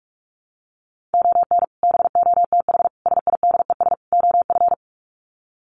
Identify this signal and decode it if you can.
Morse code, heard as a keyed tone, about 44 wpm, 700 Hz. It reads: ON BOTH SIDES OF
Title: Sets of 4 Words 45wpm